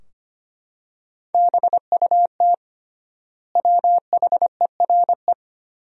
Morse code, heard as a keyed tone, about 25 wpm, 700 Hz. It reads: BUT WHERE
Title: 3x Word-Spacing - Sentences from Top 200 Words - 25wpm